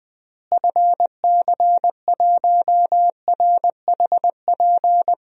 Morse code, heard as a keyed tone, about 20 wpm, 700 Hz. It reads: FC1RHP
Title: DXCC-100 Call Signs - 6 Characters - Repeated 3x - 20wpm